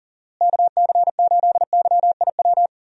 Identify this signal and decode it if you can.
Morse code, heard as a keyed tone, about 40 wpm, 700 Hz. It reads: KC8YIW
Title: USA Call Signs - 6 Characters - Repeated 3x - 40wpm